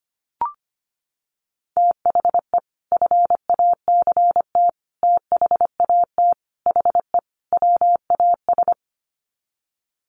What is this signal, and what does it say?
Morse code, heard as a keyed tone, about 25 wpm, 700 Hz. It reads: THE FACT THAT HE WAS
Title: Sets of 5 Words 25wpm